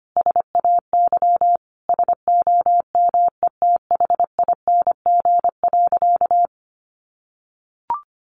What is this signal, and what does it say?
Morse code, heard as a keyed tone, about 25 wpm, 700 Hz. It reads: SAY SOMETHING.
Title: Sentences from Top 600 Words 25wpm